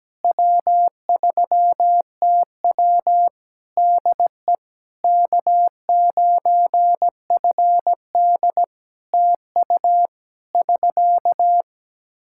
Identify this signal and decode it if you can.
Morse code, heard as a keyed tone, about 17 wpm, 700 Hz. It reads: W3TW DE K9FD TU <SK>
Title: Practice QSO - Exchange 4 - Ending - 17wpm